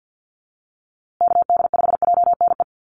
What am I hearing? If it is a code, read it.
Morse code, heard as a keyed tone, about 48 wpm, 700 Hz. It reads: KD5JNE
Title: USA Call Signs Repeated 3x 50wpm